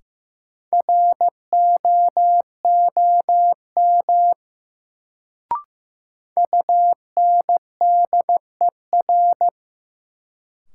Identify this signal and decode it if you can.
Morse code, heard as a keyed tone, about 15 wpm, 700 Hz. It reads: ROOM UNDER
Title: Sentences from Top 300 Words - New Word Drill - 15wpm